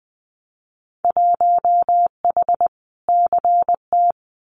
Morse code, heard as a keyed tone, about 20 wpm, 700 Hz. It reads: 1H CT
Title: Winter Field Day Exchanges 20wpm